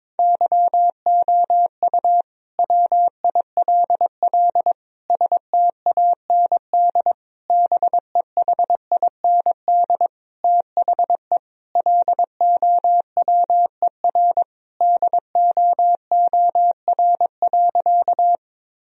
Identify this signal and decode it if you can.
Morse code, heard as a keyed tone, about 22 wpm, 700 Hz. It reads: YOU WILL STAND BEHIND THE LOWER DOOR.
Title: Sentences from Top 1000 Words - Review Part 4 - 22wpm